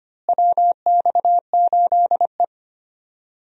Morse code, heard as a keyed tone, about 25 wpm, 700 Hz. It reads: WX8E
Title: USA Call Signs - 4 Characters - Repeated 2x - 25wpm